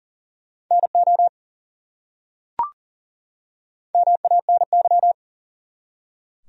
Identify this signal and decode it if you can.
Morse code, heard as a keyed tone, about 37 wpm, 700 Hz. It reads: NO MANY
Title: Top 100 Words 40wpm